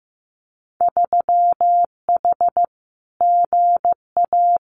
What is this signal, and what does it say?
Morse code, heard as a keyed tone, about 15 wpm, 700 Hz. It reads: 3H GA
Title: Winter Field Day Exchanges Encore 15wpm